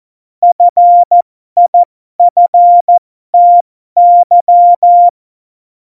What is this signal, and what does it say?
Morse code, heard as a keyed tone, about 14 wpm, spaced as for 12 wpm, 700 Hz. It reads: FIFTY
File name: Morsekurs / Witze